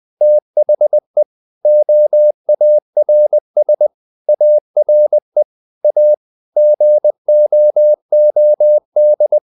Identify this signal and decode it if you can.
Morse code, heard as a keyed tone, about 20 wpm, 600 Hz. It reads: THE OARS ARE A GOOD
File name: the old man and the sea part 2 20wpm